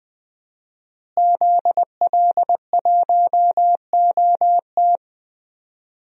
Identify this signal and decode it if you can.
Morse code, heard as a keyed tone, about 21 wpm, 700 Hz. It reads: ZL1OT
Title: DXCC-100 Call Signs - 5 Characters - Repeated 2x - 20wpm